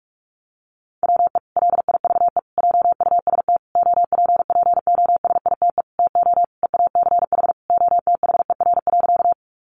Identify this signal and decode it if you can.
Morse code, heard as a keyed tone, about 44 wpm, 700 Hz. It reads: WE LIVE JUST OPPOSITE TO EACH OTHER.
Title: Sentences from Top 1300 Words 45wpm